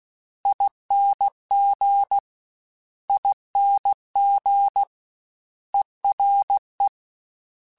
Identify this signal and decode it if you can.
Morse code, heard as a keyed tone, about 16 wpm, 800 Hz. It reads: ING ING ERE
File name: Special INGs / Special INGs-mp3s / Ere ing @15